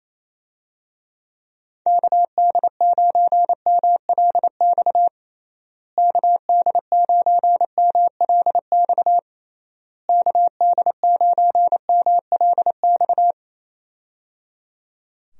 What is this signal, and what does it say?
Morse code, heard as a keyed tone, about 27 wpm, 700 Hz. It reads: KD9MLX KD9MLX KD9MLX
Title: USA Call Signs - 6 Characters - Repeated 3x - 28wpm